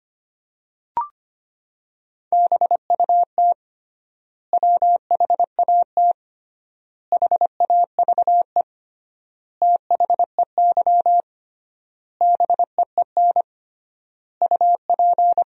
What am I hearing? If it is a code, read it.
Morse code, heard as a keyed tone, about 25 wpm, 700 Hz. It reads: BUT WHAT HAVE THEY BEEN UP
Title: 3x Word-Spacing - Sentences from Top 100 Words - 25wpm